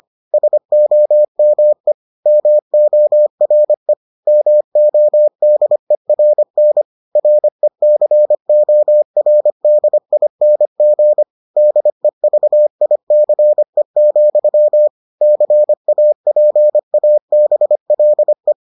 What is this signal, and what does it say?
Morse code, heard as a keyed tone, about 25 wpm, 600 Hz. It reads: SOME MORE MODERN RECORDING DEVICE, CAPABLE